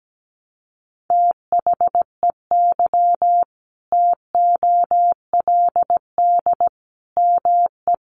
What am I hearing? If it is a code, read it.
Morse code, heard as a keyed tone, about 17 wpm, 700 Hz. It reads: THEY TOLD ME
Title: Sentences from Top 400 Words 17wpm